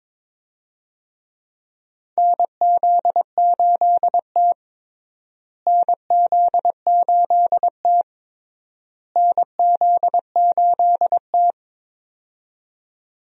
Morse code, heard as a keyed tone, about 21 wpm, 700 Hz. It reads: NZ8T NZ8T NZ8T
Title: USA Call Signs - 4 Characters - Repeated 3x - 22wpm